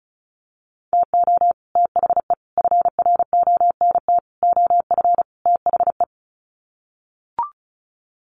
Morse code, heard as a keyed tone, about 34 wpm, 700 Hz. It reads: TO THE FRONT OF THE
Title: Sets of 5 Words 35wpm